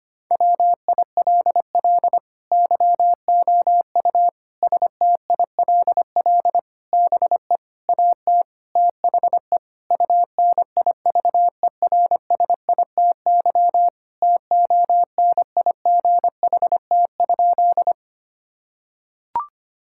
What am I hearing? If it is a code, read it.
Morse code, heard as a keyed tone, about 25 wpm, 700 Hz. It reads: WILL YOU STILL BE AT THE UNIVERSITY TONIGHT?
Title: Sentences from Top 1000 Words - Review Part 5 - 25wpm